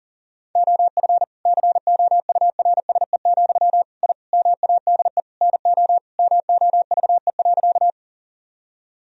Morse code, heard as a keyed tone, about 40 wpm, 700 Hz. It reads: OF COURSE, I MADE NO MOVE.
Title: Sentences from Top 1000 Words - Review Part 1 - 40wpm